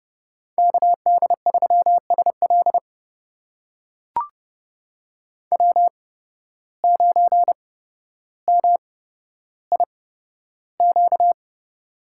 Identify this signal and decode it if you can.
Morse code, heard as a keyed tone, about 30 wpm, spaced as for 13 wpm, 700 Hz. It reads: KD3SL W9MIQ
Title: USA Call Signs - 5 Characters - 8x Character Spacing - 30wpm